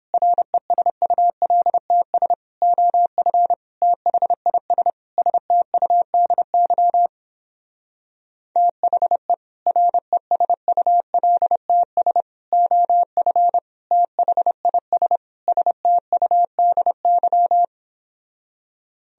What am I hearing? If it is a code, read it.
Morse code, heard as a keyed tone, about 31 wpm, 700 Hz. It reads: RESULTS OF THIS STUDY THE RESULTS OF THIS STUDY